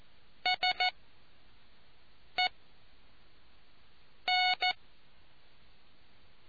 Morse code, heard as a keyed tone, about 14 wpm, spaced as for 5 wpm, 730 Hz. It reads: SE N